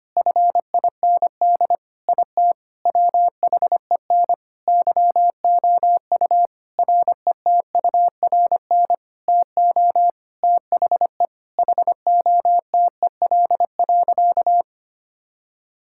Morse code, heard as a keyed tone, about 25 wpm, 700 Hz. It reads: FIND IT WHEN YOU RETURN TO THE HOTEL.
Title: Sentences from Top 1600 Words 25wpm